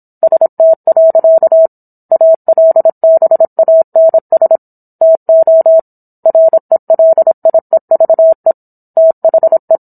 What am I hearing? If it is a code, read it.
Morse code, heard as a keyed tone, about 27 wpm, 650 Hz. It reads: ST. ALBANS TO RELIEVE THE